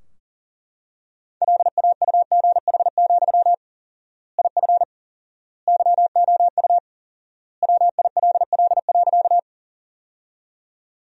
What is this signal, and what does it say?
Morse code, heard as a keyed tone, about 40 wpm, 700 Hz. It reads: LAUGH, IF YOU WILL.
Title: 4x Word-Spacing - Sentences from Top 500 Words - 40wpm